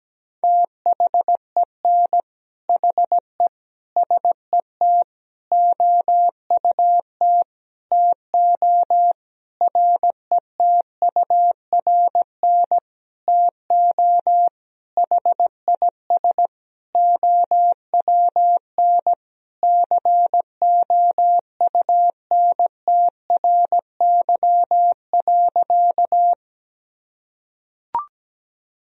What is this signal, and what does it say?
Morse code, heard as a keyed tone, about 17 wpm, 700 Hz. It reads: THEN HE SET OUT TO RETURN TO HIS OWN COUNTRY.
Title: Sentences from Top 1000 Words - Review Part 1 - 17wpm